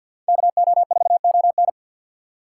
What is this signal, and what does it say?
Morse code, heard as a keyed tone, about 53 wpm, 700 Hz. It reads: KO4ON
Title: USA Call Signs - 5 Characters - 4x Character Spacing - 50wpm